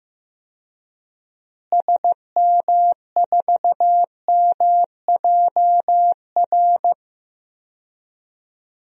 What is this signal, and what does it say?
Morse code, heard as a keyed tone, about 15 wpm, 700 Hz. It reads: SM4MJR
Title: DXCC-100 Call Signs - 6 Characters - Repeated 2x - 15wpm